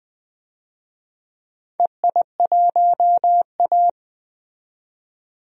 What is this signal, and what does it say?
Morse code, heard as a keyed tone, about 21 wpm, 700 Hz. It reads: EI1A
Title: DXCC-100 Call Signs - 4 Characters - 20wpm